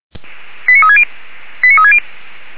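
Five-tone selective call (ZVEI), heard as quick runs of five tones, tones of about 70 ms. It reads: 89360 89360
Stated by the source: ZVEI five-tone selective call received via SDR.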